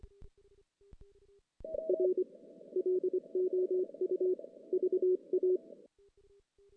Morse code, heard as a keyed tone, about 25 wpm, 410 Hz. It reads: R LOU VA
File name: 2016 JAN CW NAQP